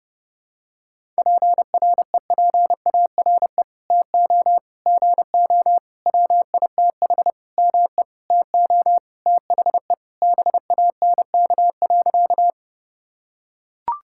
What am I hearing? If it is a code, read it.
Morse code, heard as a keyed tone, about 32 wpm, 700 Hz. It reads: PREPARE TO GO WITH ME TO THE BANK.
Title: Sentences from Top 1000 Words - Review Part 6 - 30wpm